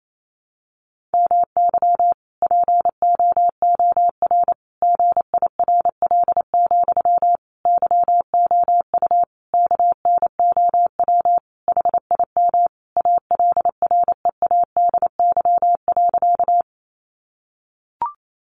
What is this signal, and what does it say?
Morse code, heard as a keyed tone, about 28 wpm, 700 Hz. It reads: MY POOR GIRL, YOU KNOW HIM ALREADY.
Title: Sentences from Top 1000 Words 28wpm